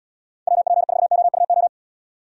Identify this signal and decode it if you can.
Morse code, heard as a keyed tone, about 71 wpm, 700 Hz. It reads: WP4OUO